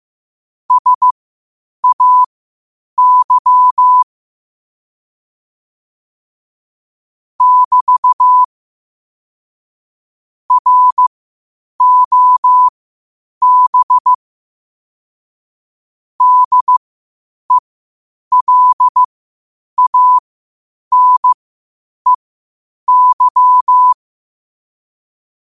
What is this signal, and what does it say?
Morse code, heard as a keyed tone, about 15 wpm, spaced as for 8 wpm, 1000 Hz. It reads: SAY <BT> ROB DELANEY
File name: qotdfiles / 8wpm